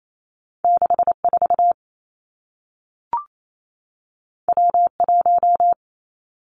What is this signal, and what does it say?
Morse code, heard as a keyed tone, about 28 wpm, 700 Hz. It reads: 6 4 W 1
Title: Two Challenging Letter-Number Combinations 28wpm